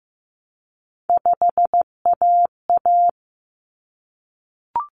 Morse code, heard as a keyed tone, about 15 wpm, 700 Hz. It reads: 5AA
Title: Common Cut Numbers - 3-Digit 15wpm